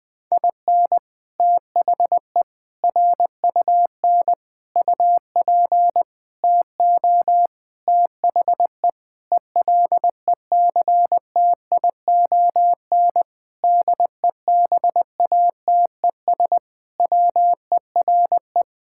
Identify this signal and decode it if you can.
Morse code, heard as a keyed tone, about 20 wpm, 700 Hz. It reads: IN THE RUN UP TO THE ELECTION DEBATES WERE
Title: Sentences with 6-Word Phrases - 20wpm